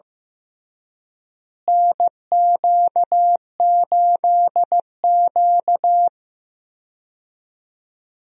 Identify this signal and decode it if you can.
Morse code, heard as a keyed tone, about 15 wpm, 700 Hz. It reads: NQ8Q
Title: USA Call Signs - 4 Characters - Repeated 3x - 15wpm